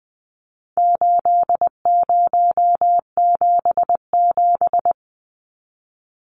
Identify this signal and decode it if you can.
Morse code, heard as a keyed tone, about 20 wpm, 700 Hz. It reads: 8077
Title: Numbers - 4-Digit 20wpm